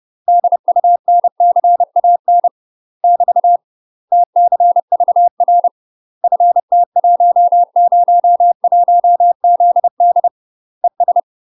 Morse code, heard as a keyed tone, about 30 wpm, 700 Hz. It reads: DUNCAN <BT> TCVR FT101ZD ES
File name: UK MASTER QSOs TRAINING / 30WPM / UK MASTER QSOs 0019